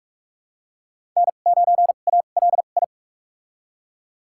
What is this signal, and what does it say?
Morse code, heard as a keyed tone, about 43 wpm, spaced as for 32 wpm, 700 Hz. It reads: N9ALI
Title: USA Call Signs - 5 Characters - 2x Character Spacing - 45wpm